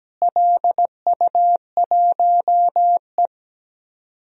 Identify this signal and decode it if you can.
Morse code, heard as a keyed tone, about 17 wpm, 700 Hz. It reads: LU1E
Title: DXCC-100 Call Signs - 4 Characters - Repeated 3x - 17wpm